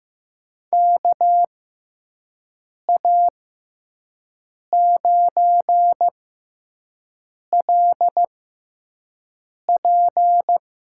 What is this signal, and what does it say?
Morse code, heard as a keyed tone, about 15 wpm, spaced as for 5 wpm, 700 Hz. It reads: KA9LP
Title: USA Call Signs - 6 Characters - 6x Character Spacing - 15wpm